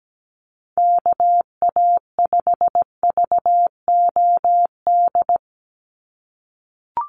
Morse code, heard as a keyed tone, about 17 wpm, 700 Hz. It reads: KA5VOD
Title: USA Call Signs 17wpm